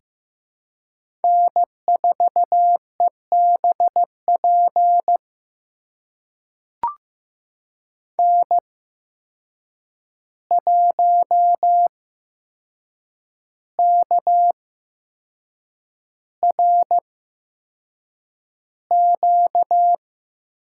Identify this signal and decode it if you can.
Morse code, heard as a keyed tone, about 15 wpm, spaced as for 6 wpm, 700 Hz. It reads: N4EBP N1KRQ
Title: USA Call Signs - 5 Characters - 8x Character Spacing - 15wpm